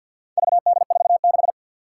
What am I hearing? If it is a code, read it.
Morse code, heard as a keyed tone, about 46 wpm, 700 Hz. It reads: VD46
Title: Four Challenging Letter-Number Combinations 50wpm